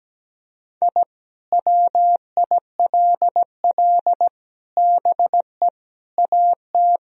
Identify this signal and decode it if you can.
Morse code, heard as a keyed tone, about 17 wpm, 700 Hz. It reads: I WILL BE AT
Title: Sentences from Top 1000 Words - Review Part 3 - 17wpm